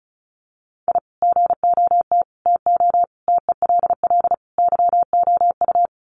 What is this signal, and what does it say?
Morse code, heard as a keyed tone, about 34 wpm, 700 Hz. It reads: I GOT TO TELL YOU
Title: Sets of 5 Words Encore2 35wpm